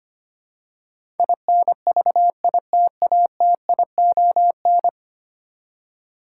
Morse code, heard as a keyed tone, about 25 wpm, 700 Hz. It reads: INVITATION
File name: Ten Letter Words - 25wpm